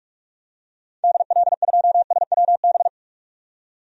Morse code, heard as a keyed tone, about 46 wpm, 700 Hz. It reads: DL2SWB
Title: DXCC-100 Call Signs - 6 Characters - 45wpm